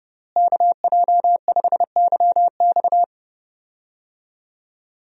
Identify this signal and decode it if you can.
Morse code, heard as a keyed tone, about 32 wpm, 700 Hz. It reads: KJ5YX
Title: USA Call Signs - 5 Characters - Repeated 2x - 30wpm